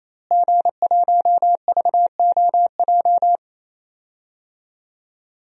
Morse code, heard as a keyed tone, about 28 wpm, 700 Hz. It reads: G1VOJ
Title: DXCC-100 Call Signs - 5 Characters - Repeated 3x - 28wpm